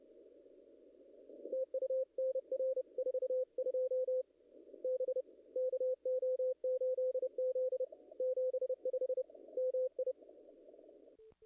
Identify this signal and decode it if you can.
Morse code, heard as a keyed tone, about 30 wpm, 500 Hz. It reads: TU NR 42 B KO8Z 75 MI
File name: contests / ss-cw-k5tr-op-n5rz -0000